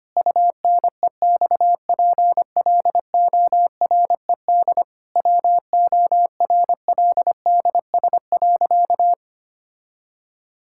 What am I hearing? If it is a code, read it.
Morse code, heard as a keyed tone, about 25 wpm, 700 Hz. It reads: UNEXPLORED WORLDS.
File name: practice / 25wpm / Difficult Sentences with Hints - 25wpm